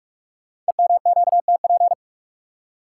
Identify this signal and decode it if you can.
Morse code, heard as a keyed tone, about 44 wpm, 690 Hz. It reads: EMQTP
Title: Code Groups - 5 Letters - 45wpm